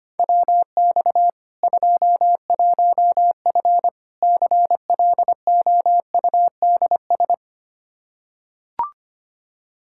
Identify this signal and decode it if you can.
Morse code, heard as a keyed tone, about 25 wpm, 700 Hz. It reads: WX 21F CLOUDS
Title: K1USN SST QSO Practice 25wpm